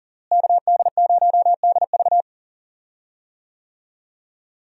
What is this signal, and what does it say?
Morse code, heard as a keyed tone, about 38 wpm, 700 Hz. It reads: KD0DV
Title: USA Call Signs - 5 Characters - Repeated 3x - 40wpm